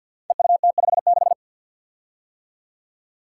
Morse code, heard as a keyed tone, about 51 wpm, 700 Hz. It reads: EUT56
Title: Code Groups - 5 Letter:Numbers - 50wpm